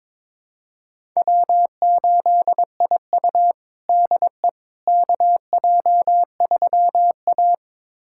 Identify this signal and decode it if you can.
Morse code, heard as a keyed tone, about 22 wpm, 700 Hz. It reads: W8IU DE KJ3A
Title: USA Callsign DE Callsign - 4 Characters - 22wpm